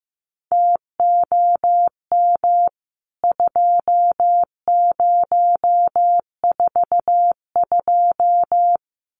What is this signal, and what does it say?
Morse code, heard as a keyed tone, about 15 wpm, 700 Hz. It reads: TOM 2042
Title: CWT Exchange 15wpm